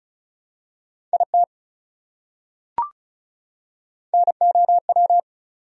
Morse code, heard as a keyed tone, about 34 wpm, 700 Hz. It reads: IT NOW
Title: Top 100 Words 35wpm